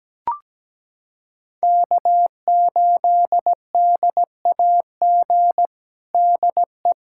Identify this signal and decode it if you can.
Morse code, heard as a keyed tone, about 17 wpm, 700 Hz. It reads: K8DAG DE
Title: USA Callsign DE Callsign - 5 Characters - 17wpm